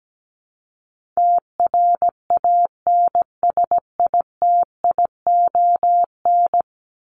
Morse code, heard as a keyed tone, about 17 wpm, 700 Hz. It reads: TRANSITION
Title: Top 1751-2000 Words 17wpm